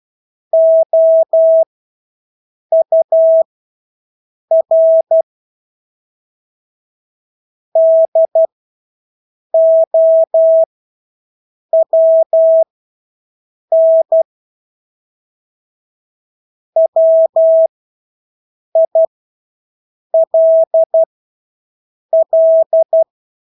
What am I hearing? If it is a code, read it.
Morse code, heard as a keyed tone, about 12 wpm, spaced as for 6 wpm, 650 Hz. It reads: OUR DOWN WILL
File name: Engelske ord 12-6 wpm